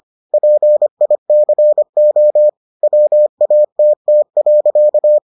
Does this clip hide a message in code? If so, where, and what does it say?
Yes: Morse code from 0.3 to 5.2 s, heard as a keyed tone, about 25 wpm, 600 Hz. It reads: PICO WATT.